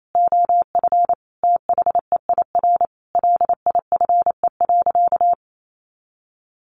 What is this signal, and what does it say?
Morse code, heard as a keyed tone, about 28 wpm, 700 Hz. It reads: OF THEIR LIFE.
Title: Sentences from Top 1200 Words 28wpm